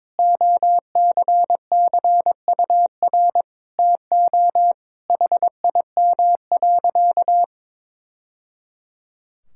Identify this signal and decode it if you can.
Morse code, heard as a keyed tone, about 21 wpm, 700 Hz. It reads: OCCUR TO HIM.
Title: Sentences from Top 1000 Words - Review Part 6 - 22wpm